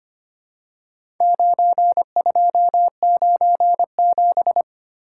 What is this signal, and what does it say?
Morse code, heard as a keyed tone, about 25 wpm, 700 Hz. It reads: 9297
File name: practice / 25wpm / Four Letter-Number Combinations - 25wpm